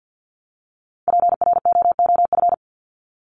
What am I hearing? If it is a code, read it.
Morse code, heard as a keyed tone, about 50 wpm, 700 Hz. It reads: PROOF
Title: Top 1501-1750 Words 50wpm